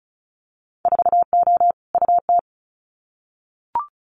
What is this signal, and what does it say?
Morse code, heard as a keyed tone, about 36 wpm, 700 Hz. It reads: 4O UT
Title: Winter Field Day Exchanges Encore 35wpm